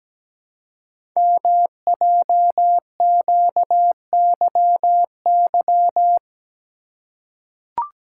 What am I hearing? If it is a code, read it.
Morse code, heard as a keyed tone, about 17 wpm, 700 Hz. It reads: MJQYY
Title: Code Groups - 5 Letters - 17wpm